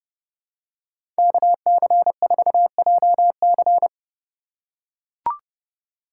Morse code, heard as a keyed tone, about 30 wpm, 700 Hz. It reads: KC4JC
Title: USA Call Signs - 5 Characters - 2x Character Spacing - 30wpm